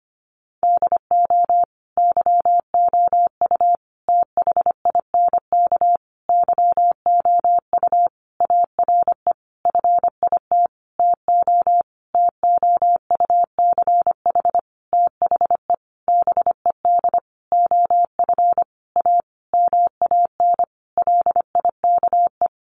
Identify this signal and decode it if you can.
Morse code, heard as a keyed tone, about 25 wpm, 700 Hz. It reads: DO YOU THINK YOU ARE FIT TO TOUCH THE BED OF A MAN LIKE
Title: Sentences from Top 1200 Words 25wpm